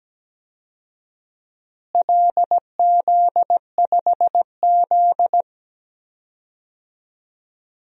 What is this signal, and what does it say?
Morse code, heard as a keyed tone, about 17 wpm, 700 Hz. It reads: LZ5Z
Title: DXCC-100 Call Signs - 4 Characters - Repeated 3x - 17wpm